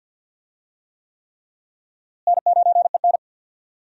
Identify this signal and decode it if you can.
Morse code, heard as a keyed tone, about 53 wpm, 700 Hz. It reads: N9EN
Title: USA Call Signs - 4 Characters - Repeated 3x - 50wpm